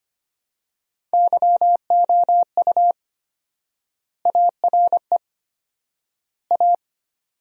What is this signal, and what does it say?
Morse code, heard as a keyed tone, about 25 wpm, 700 Hz. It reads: YOU ARE A
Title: 4x Word-Spacing - Sentences from Top 400 Words - 25wpm